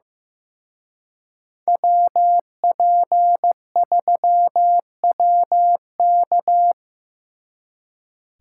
Morse code, heard as a keyed tone, about 18 wpm, 700 Hz. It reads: WP3WK
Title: Speed Racing - USA 2x2 Call Signs 15wpm